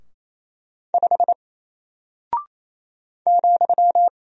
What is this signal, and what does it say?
Morse code, heard as a keyed tone, about 28 wpm, 700 Hz. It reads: 5 ,
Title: Single Character (Letter:Number:Punctuation:Prosign) Alt 28wpm